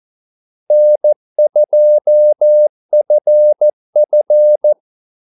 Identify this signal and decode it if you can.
Morse code, heard as a keyed tone, about 14 wpm, 600 Hz. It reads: N2FF
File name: Callsigns-14wpm